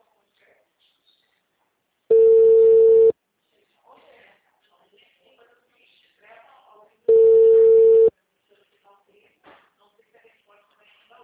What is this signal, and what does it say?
Ringback tone (China, 450 Hz), with speech.
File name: out- -2034- - - .61432